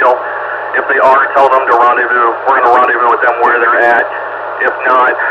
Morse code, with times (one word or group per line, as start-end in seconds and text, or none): none